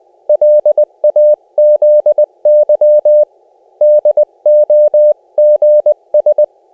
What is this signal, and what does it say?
Morse code, heard as a keyed tone, about 20 wpm, 600 Hz. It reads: LAZY DOGS
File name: cw / cwfox / Op C 600 Hz